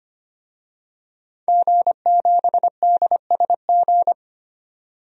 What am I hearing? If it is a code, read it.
Morse code, heard as a keyed tone, about 25 wpm, 700 Hz. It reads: G7DSG
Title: DXCC-100 Call Signs - 5 Characters - 25wpm